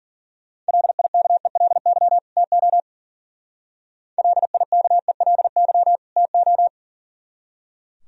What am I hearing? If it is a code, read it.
Morse code, heard as a keyed tone, about 50 wpm, 700 Hz. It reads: LIKELY TO LIKELY TO
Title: Speed-Racing - Sets of 2 Words 40wpm